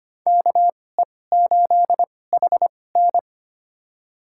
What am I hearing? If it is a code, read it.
Morse code, heard as a keyed tone, about 25 wpm, spaced as for 18 wpm, 700 Hz. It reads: KE8HN
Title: USA Call Signs - 5 Characters - 2x Character Spacing - 25wpm